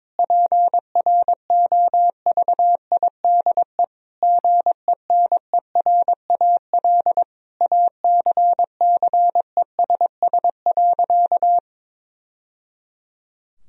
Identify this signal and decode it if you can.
Morse code, heard as a keyed tone, about 21 wpm, 700 Hz. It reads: PROVIDE GENERAL ACCESS.
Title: Sentences from Top 1000 Words - Review Part 4 - 22wpm